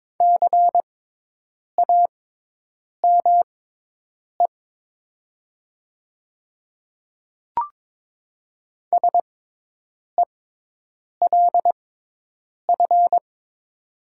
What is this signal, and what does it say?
Morse code, heard as a keyed tone, about 22 wpm, spaced as for 8 wpm, 700 Hz. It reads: CAME SELF
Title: 6x Character Spacing - Top 200 Words - 22wpm